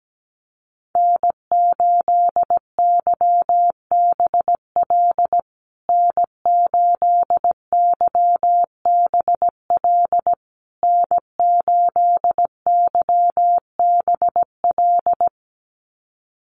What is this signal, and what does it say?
Morse code, heard as a keyed tone, about 17 wpm, 700 Hz. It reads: N8YBL N8YBL N8YBL
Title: USA Call Signs Repeated 3x 17wpm